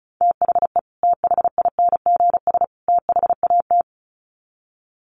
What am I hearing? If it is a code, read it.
Morse code, heard as a keyed tone, about 34 wpm, 700 Hz. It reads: THE THINGS THAT
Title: Sets of 3 Words 35wpm